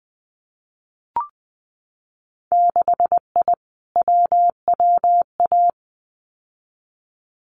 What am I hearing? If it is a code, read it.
Morse code, heard as a keyed tone, about 20 wpm, 700 Hz. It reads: 6I WWA
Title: Winter Field Day Exchanges Encore 20wpm